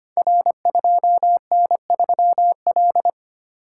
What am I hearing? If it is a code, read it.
Morse code, heard as a keyed tone, about 25 wpm, 700 Hz. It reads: R2N3L
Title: Code Groups - 5 Letter:Numbers - 25wpm